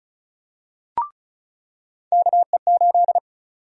Morse code, heard as a keyed tone, about 36 wpm, 700 Hz. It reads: KE8
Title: USA Call Sign Prefixes - 35wpm